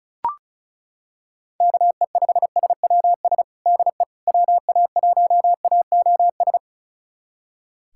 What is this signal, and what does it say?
Morse code, heard as a keyed tone, about 34 wpm, 700 Hz. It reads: KE5SWS DE WA1AOS
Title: USA Callsign DE Callsign - 6 Characters - 35wpm